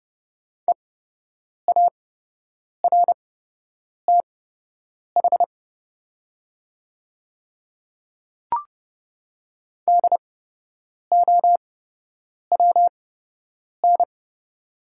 Morse code, heard as a keyed tone, about 31 wpm, spaced as for 8 wpm, 700 Hz. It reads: EARTH DOWN
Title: 8x Character Spacing - Top 300 Words - 30wpm